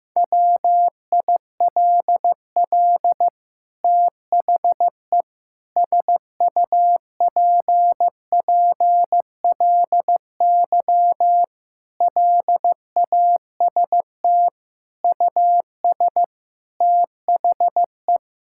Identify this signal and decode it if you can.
Morse code, heard as a keyed tone, about 15 wpm, 700 Hz. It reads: WILL THE SUPPLY LAST US THE
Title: Sentences from Top 1000 Words - Review Part 5 - 15wpm